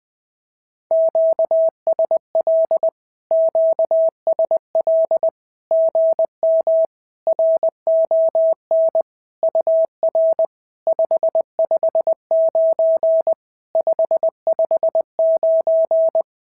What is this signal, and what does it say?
Morse code, heard as a keyed tone, about 20 wpm, 650 Hz. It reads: QSL QSL GM RON UR 559 559
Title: SOTA QSO Practice 20wpm